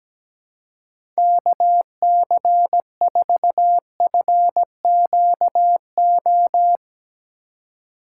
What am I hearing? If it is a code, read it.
Morse code, heard as a keyed tone, about 17 wpm, 700 Hz. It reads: KC4FQO
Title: USA Call Signs - 6 Characters - 3x Character Spacing - 17wpm